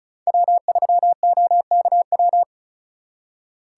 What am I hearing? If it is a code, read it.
Morse code, heard as a keyed tone, about 45 wpm, 700 Hz. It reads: W3OKW